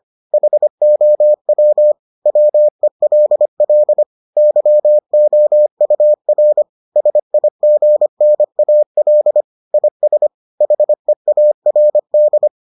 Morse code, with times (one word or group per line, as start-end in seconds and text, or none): 0.3-1.9 HOW
2.3-4.0 WELL
4.4-6.6 YOUR
7.0-9.4 SIGNAL
9.7-10.3 IS
10.6-12.5 HEARD